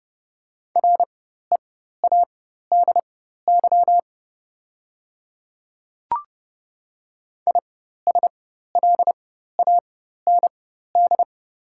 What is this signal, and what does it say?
Morse code, heard as a keyed tone, about 30 wpm, spaced as for 14 wpm, 690 Hz. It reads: READY ISLAND
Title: 4x Character Spacing - Top 500 Words - 30wpm